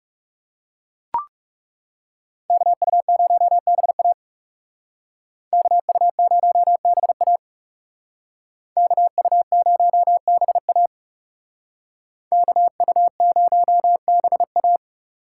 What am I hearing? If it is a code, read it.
Morse code, heard as a keyed tone, about 38 wpm, 700 Hz. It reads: KU0BA KU0BA KU0BA KU0BA